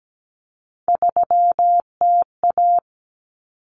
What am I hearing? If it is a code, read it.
Morse code, heard as a keyed tone, about 17 wpm, 700 Hz. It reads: 3TA
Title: Common Cut Numbers - 3-Digit 17wpm